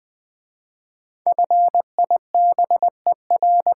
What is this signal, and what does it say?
Morse code, heard as a keyed tone, about 20 wpm, 700 Hz. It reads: FIBER
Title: 5 Letter Words - First 3 Letters Given - 20wpm